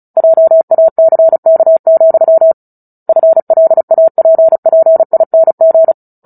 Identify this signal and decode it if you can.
Morse code, heard as a keyed tone, about 34 wpm, 650 Hz. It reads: JACK, FLAPPING
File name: Book Two / 09-Chapter 9 - 35 WPM